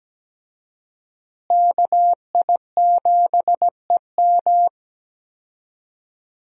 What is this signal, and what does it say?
Morse code, heard as a keyed tone, about 17 wpm, 700 Hz. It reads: KI7EM
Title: USA Call Signs - 5 Characters - Repeated 2x - 17wpm